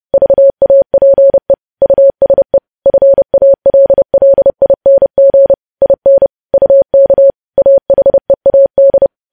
Morse code, heard as a keyed tone, about 30 wpm, 550 Hz. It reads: VAPE USE FALLING IN UK AHEAD
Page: news headlines in morse code at 30 wpm